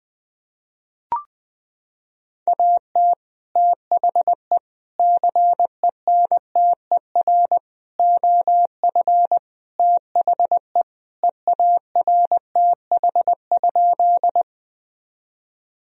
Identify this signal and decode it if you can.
Morse code, heard as a keyed tone, about 21 wpm, 700 Hz. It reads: AT THE CENTER OF THE EARTH?
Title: Sentences from Top 1000 Words - Review Part 2 - 20wpm